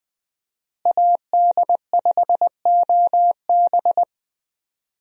Morse code, heard as a keyed tone, about 20 wpm, 700 Hz. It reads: AD5OB
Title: USA Call Signs - 5 Characters - Repeated 2x - 20wpm